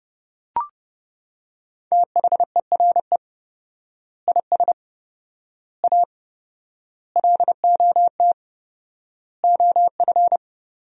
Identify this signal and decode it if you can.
Morse code, heard as a keyed tone, about 32 wpm, 700 Hz. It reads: THERE IS A LOT OF
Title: 4x Word-Spacing - Sentences from Top 500 Words - 30wpm